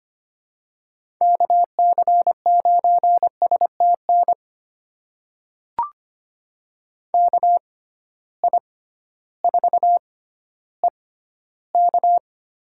Morse code, heard as a keyed tone, about 25 wpm, spaced as for 13 wpm, 700 Hz. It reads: KC9STN KI4EK
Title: USA Call Signs - 6 Characters - 6x Character Spacing - 25wpm